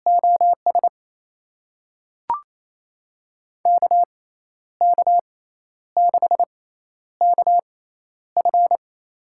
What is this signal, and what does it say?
Morse code, heard as a keyed tone, about 28 wpm, spaced as for 11 wpm, 700 Hz. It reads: OS KK6KF
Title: USA Call Signs - 5 Characters - 6x Character Spacing - 28wpm